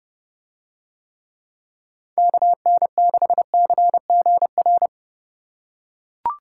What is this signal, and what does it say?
Morse code, heard as a keyed tone, about 28 wpm, 700 Hz. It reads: KN6CGR